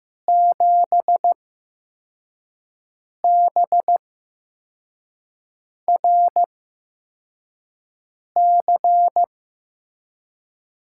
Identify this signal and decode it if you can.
Morse code, heard as a keyed tone, about 15 wpm, 700 Hz. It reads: 7 B R C
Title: USA Call Signs - 8x Character Spacing - 15wpm